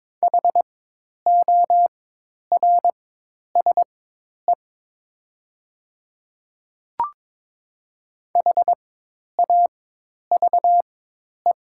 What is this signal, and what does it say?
Morse code, heard as a keyed tone, about 22 wpm, spaced as for 10 wpm, 700 Hz. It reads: HORSE HAVE
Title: 4x Character Spacing - Top 500 Words - 22wpm